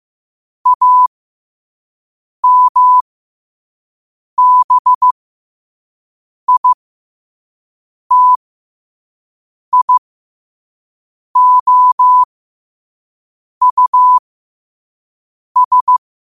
Morse code, heard as a keyed tone, about 15 wpm, spaced as for 5 wpm, 1000 Hz. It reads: AMBITIOUS